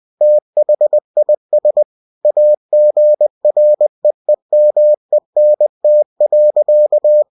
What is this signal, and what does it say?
Morse code, heard as a keyed tone, about 20 wpm, 600 Hz. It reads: THIS AGREEMENT.
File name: cw-books / winnie-the-pooh / winnie-20wpm-page-0113